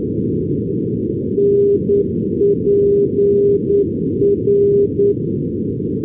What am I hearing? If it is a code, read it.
Morse code, heard as a keyed tone, about 9 wpm, 400 Hz. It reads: NPR
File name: europe / pol / pol-415 np pol npr-488-2015